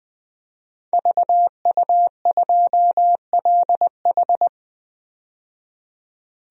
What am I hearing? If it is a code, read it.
Morse code, heard as a keyed tone, about 20 wpm, 700 Hz. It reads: VU2LH
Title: DXCC-100 Call Signs - 5 Characters - 20wpm